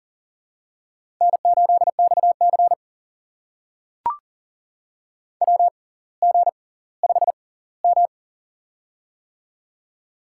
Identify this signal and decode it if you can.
Morse code, heard as a keyed tone, about 40 wpm, spaced as for 20 wpm, 700 Hz. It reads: N8XC WG5M
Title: USA Call Signs - 6x Character Spacing - 40wpm